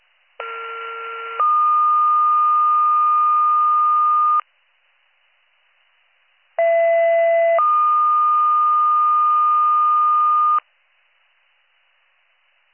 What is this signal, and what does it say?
Two-tone page, heard as a one-second tone, then a three-second tone, about 1370 then 1220 Hz, about 670 then 1220 Hz. It is heard from 0.4 to 10.6 s.